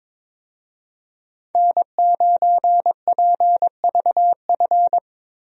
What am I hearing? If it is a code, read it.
Morse code, heard as a keyed tone, about 22 wpm, 700 Hz. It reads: N9PVF
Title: USA Call Signs - 5 Characters - Repeated 2x - 22wpm